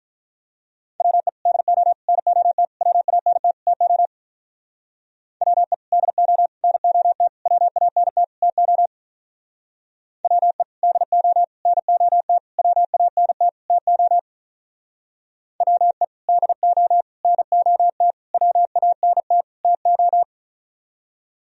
Morse code, heard as a keyed tone, about 45 wpm, 700 Hz. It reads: WE DO NOT WANT TO WE DO NOT WANT TO WE DO NOT WANT TO WE DO NOT WANT TO